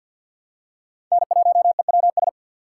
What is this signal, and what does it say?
Morse code, heard as a keyed tone, about 46 wpm, 700 Hz. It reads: N1EWS
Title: USA Call Signs - 5 Characters - 4x Character Spacing - 50wpm